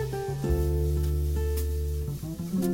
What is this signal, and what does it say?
Dial tone (North America), 350 + 440 Hz.